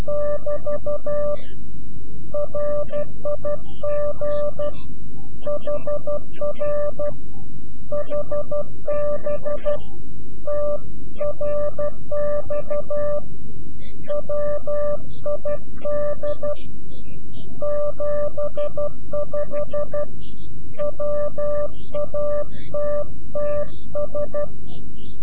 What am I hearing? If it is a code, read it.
Morse code, heard as a keyed tone, about 12 wpm, 590 Hz. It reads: <BT> RIG HR HB TRX WID 75 WATTS
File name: CW-QSO / CQ Call / CQ-QSO